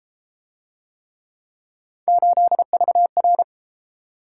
Morse code, heard as a keyed tone, about 31 wpm, 700 Hz. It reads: 8VR